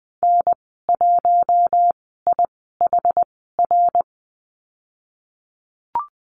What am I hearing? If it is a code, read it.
Morse code, heard as a keyed tone, about 20 wpm, spaced as for 15 wpm, 700 Hz. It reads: N1IHR
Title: 069 Lesson 12g - Letter d - Call Signs 20wpm